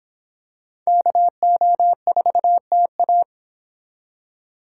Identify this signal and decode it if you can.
Morse code, heard as a keyed tone, about 26 wpm, 700 Hz. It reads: KO4TA